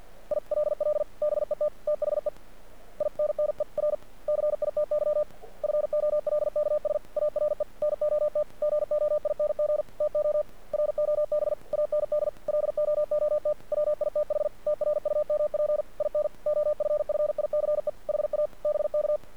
Morse code, heard as a keyed tone, about 46 wpm, 600 Hz. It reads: ILL BET THE INNER CITY FOLKS ARE NOT GOING TO ROB AND LOOT WITH TRUMP IN OFFICE HA BK
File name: archived / CRSnet